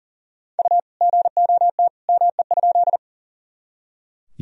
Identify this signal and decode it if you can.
Morse code, heard as a keyed tone, about 40 wpm, 700 Hz. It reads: U GOT ME?
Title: Sentences from Top 300 Words and Abbreviations 40wpm